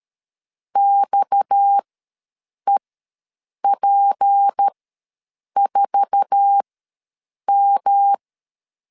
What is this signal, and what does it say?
Morse code, heard as a keyed tone, about 13 wpm, spaced as for 7 wpm, 800 Hz. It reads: XEP4M
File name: LEZIONI / 8 YJ45 / 8 YJ4 ws 12wpm 3249